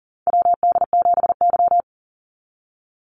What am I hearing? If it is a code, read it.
Morse code, heard as a keyed tone, about 37 wpm, 700 Hz. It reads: WD7Y
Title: NA SOTA Chaser Callsigns 40wpm